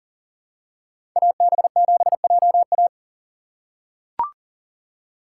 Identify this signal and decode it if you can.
Morse code, heard as a keyed tone, about 40 wpm, 700 Hz. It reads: AB7JA
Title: USA Call Signs - 3x Character Spacing - 40wpm